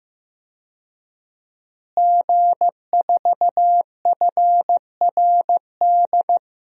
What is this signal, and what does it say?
Morse code, heard as a keyed tone, about 15 wpm, 700 Hz. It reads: G4FRD
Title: DXCC-100 Call Signs - 5 Characters - Repeated 2x - 15wpm